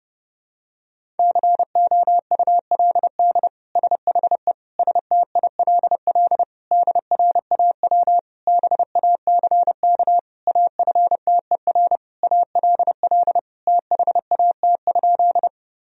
Morse code, heard as a keyed tone, about 30 wpm, 700 Hz. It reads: COULD SHE STILL DRAW BACK AFTER ALL THAT?
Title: Sentences from Top 1000 Words - Review Part 1 - 30wpm